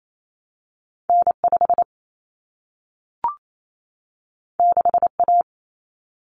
Morse code, heard as a keyed tone, about 28 wpm, 700 Hz. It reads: N5 6A
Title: Common Cut Numbers - 2-Digit 28wpm